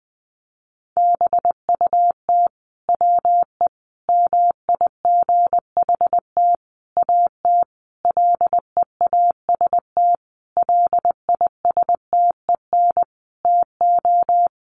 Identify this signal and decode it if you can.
Morse code, heard as a keyed tone, about 20 wpm, 700 Hz. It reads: BUT WE MIGHT AT LEAST LISTEN TO
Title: Sentences from Top 1300 Words 20wpm